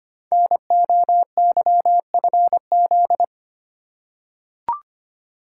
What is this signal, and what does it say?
Morse code, heard as a keyed tone, about 25 wpm, 700 Hz. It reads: NOYFZ
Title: Code Groups - 5 Letters - 25wpm